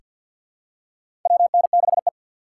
Morse code, heard as a keyed tone, about 47 wpm, 690 Hz. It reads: WN6E
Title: USA Call Signs - 4 Characters - 6x Character Spacing - 50wpm